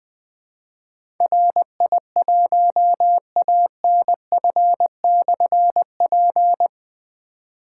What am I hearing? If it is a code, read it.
Morse code, heard as a keyed tone, about 20 wpm, 700 Hz. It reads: RI1ANF/P
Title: Difficult Call Signs 20wpm